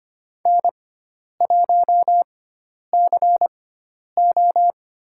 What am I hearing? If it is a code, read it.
Morse code, heard as a keyed tone, about 25 wpm, 700 Hz. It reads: N 1 C O
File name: Lesson 16g - Letter c - Call Signs 25wpm